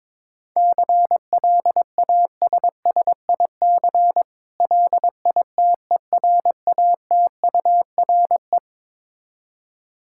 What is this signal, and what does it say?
Morse code, heard as a keyed tone, about 21 wpm, 700 Hz. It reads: CLASSIC LITERATURE
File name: Difficult Words with Hints - 22wpm